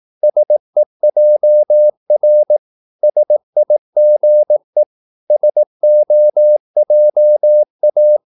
Morse code, heard as a keyed tone, about 18 wpm, 600 Hz. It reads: SEJR SIGE SOJA
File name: Korte ord DK 18wpm